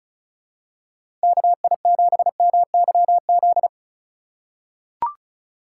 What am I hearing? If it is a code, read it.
Morse code, heard as a keyed tone, about 36 wpm, 700 Hz. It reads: KI7MYZ
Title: USA Call Signs - 3x Character Spacing - 35wpm